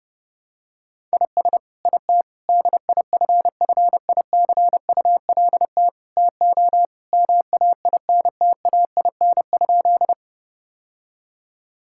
Morse code, heard as a keyed tone, about 32 wpm, 700 Hz. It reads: IS IT DIFFICULT TO MAINTAIN?
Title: Sentences from Top 1000 Words - Review Part 5 - 30wpm